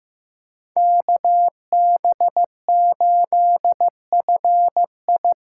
Morse code, heard as a keyed tone, about 15 wpm, 700 Hz. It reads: KB8FI
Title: USA Call Signs - 6 Characters - 8x Character Spacing - 15wpm